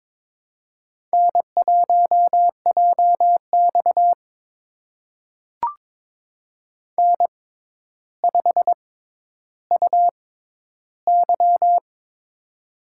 Morse code, heard as a keyed tone, about 22 wpm, spaced as for 11 wpm, 700 Hz. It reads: N1JX N5UY
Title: USA Call Signs - 4 Characters - 6x Character Spacing - 22wpm